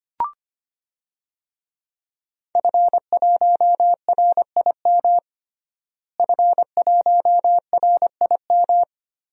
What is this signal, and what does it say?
Morse code, heard as a keyed tone, about 25 wpm, 700 Hz. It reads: F1RIM F1RIM
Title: DXCC-100 Call Signs - 5 Characters - Repeated 2x - 25wpm